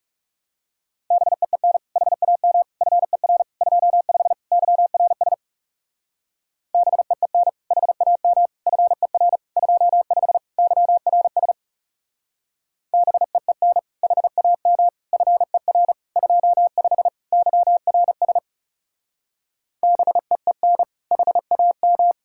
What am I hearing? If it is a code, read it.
Morse code, heard as a keyed tone, about 36 wpm, 700 Hz. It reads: BEEN HAM FER 25 YRS BEEN HAM FER 25 YRS BEEN HAM FER 25 YRS BEEN HAM
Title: Speed-Racing Common QSO Phrases Encore - 30wpm